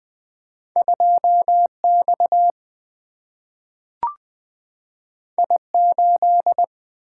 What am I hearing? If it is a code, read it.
Morse code, heard as a keyed tone, about 20 wpm, 700 Hz. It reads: 2 X I 8
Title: Single-Letter Single-Number 20wpm